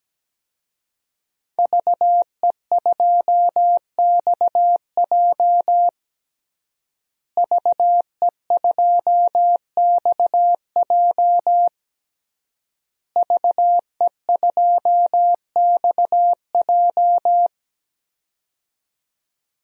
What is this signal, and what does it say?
Morse code, heard as a keyed tone, about 17 wpm, 700 Hz. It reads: VE2XJ VE2XJ VE2XJ
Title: DXCC-100 Call Signs - 5 Characters - Repeated 3x - 17wpm